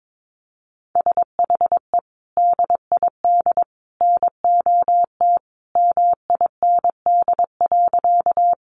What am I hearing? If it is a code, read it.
Morse code, heard as a keyed tone, about 22 wpm, 700 Hz. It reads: SHE DID NOT MIND.
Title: Sentences from Top 600 Words 22wpm